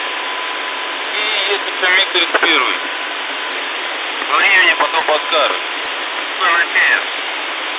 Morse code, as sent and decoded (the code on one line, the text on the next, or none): none